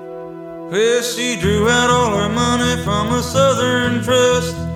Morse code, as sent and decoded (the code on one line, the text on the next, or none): none